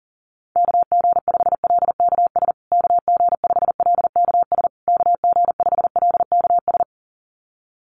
Morse code, heard as a keyed tone, about 40 wpm, 700 Hz. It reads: KG5LKS KG5LKS KG5LKS
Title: USA Call Signs Repeated 3x 40wpm